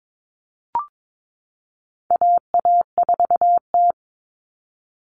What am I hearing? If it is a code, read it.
Morse code, heard as a keyed tone, about 22 wpm, 700 Hz. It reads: AA4T
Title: All Cut Numbers - 4-Digit 22wpm